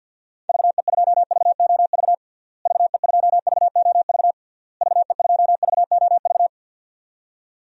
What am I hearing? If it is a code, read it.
Morse code, heard as a keyed tone, about 52 wpm, 700 Hz. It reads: VE2VOV VE2VOV VE2VOV
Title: DXCC-100 Call Signs - 6 Characters - Repeated 3x - 50wpm